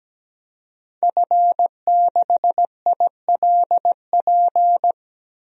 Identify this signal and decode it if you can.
Morse code, heard as a keyed tone, about 17 wpm, 700 Hz. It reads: F6ILP
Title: DXCC-100 Call Signs - 5 Characters - 17wpm